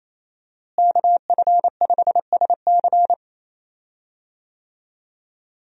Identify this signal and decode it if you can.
Morse code, heard as a keyed tone, about 28 wpm, 700 Hz. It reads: KF5SC
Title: USA Call Signs - 5 Characters - Repeated 2x - 28wpm